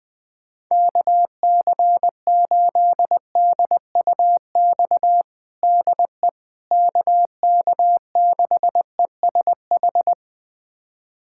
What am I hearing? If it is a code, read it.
Morse code, heard as a keyed tone, about 20 wpm, 700 Hz. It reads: KC8DUX DE KK6ESH
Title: USA Callsign DE Callsign 20wpm